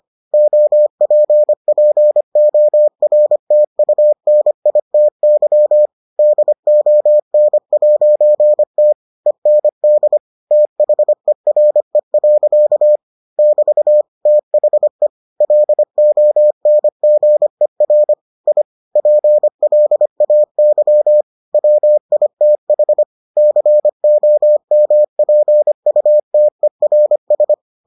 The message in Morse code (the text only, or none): OPPORTUNITY DON'T END THERE. <BT> THE LONGER I PLAY WITH COMPUTERS